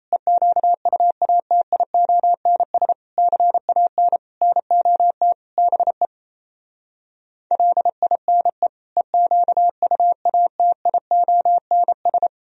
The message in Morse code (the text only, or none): EQUATIONS CAN NOT BE LINE EQUATIONS